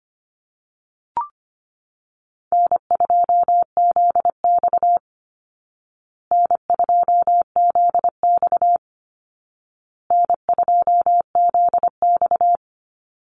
Morse code, heard as a keyed tone, about 25 wpm, 700 Hz. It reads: N2ZX N2ZX N2ZX
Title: Easy Call Signs Repeated 3x - 25wpm